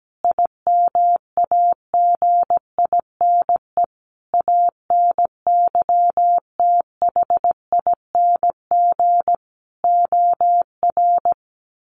Morse code, heard as a keyed tone, about 17 wpm, 700 Hz. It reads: IMAGINE ANYTHING OR
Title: Sentences from Top 1400 Words 17wpm